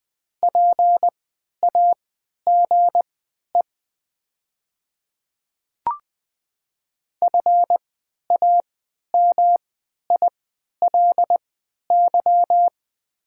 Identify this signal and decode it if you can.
Morse code, heard as a keyed tone, about 20 wpm, spaced as for 11 wpm, 700 Hz. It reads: PAGE FAMILY
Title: 3x Character Spacing - Top 500 Words - 20wpm